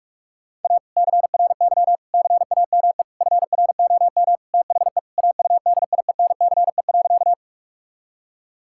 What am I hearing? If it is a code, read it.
Morse code, heard as a keyed tone, about 41 wpm, 700 Hz. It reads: A CRY CAME FROM THE AUDIENCE.
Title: Sentences from Top 1000 Words - Review Part 5 - 45wpm